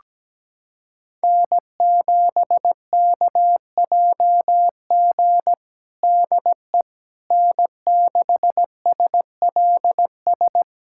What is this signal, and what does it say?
Morse code, heard as a keyed tone, about 17 wpm, 700 Hz. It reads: N7KJG DE N6SLS
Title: USA Callsign DE Callsign - 5 Characters - 17wpm